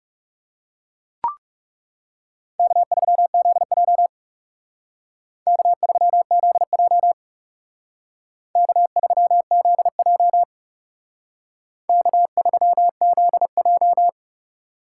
Morse code, heard as a keyed tone, about 36 wpm, 700 Hz. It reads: K3ZJ K3ZJ K3ZJ K3ZJ
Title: Speed Racing - USA 1x2 Call Signs 30wpm